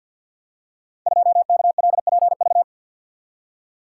Morse code, heard as a keyed tone, about 51 wpm, 700 Hz. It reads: 2KLPV
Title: Code Groups - 5 Letter:Numbers - 50wpm